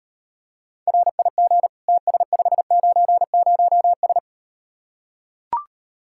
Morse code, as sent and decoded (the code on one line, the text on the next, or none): .-. .. --. / - ... ..... ----. ----- ...
RIG TS590S